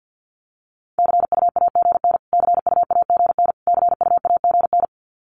Morse code, heard as a keyed tone, about 50 wpm, 700 Hz. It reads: CUAGN CUAGN CUAGN
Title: Abbreviations Repeated 3x - 50wpm